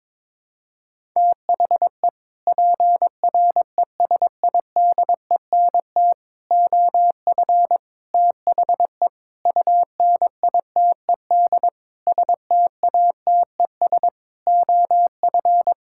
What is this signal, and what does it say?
Morse code, heard as a keyed tone, about 22 wpm, 700 Hz. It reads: THE PRESIDENT OF THE UNITED STATES OF
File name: practice / 22wpm / Sets of 8 Words - 22wpm